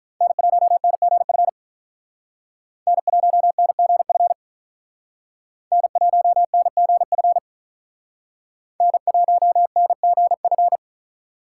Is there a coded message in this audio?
Yes: Morse code from 0.2 to 10.8 s, heard as a keyed tone, about 45 wpm, 700 Hz. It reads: N1NGF N1NGF N1NGF N1NGF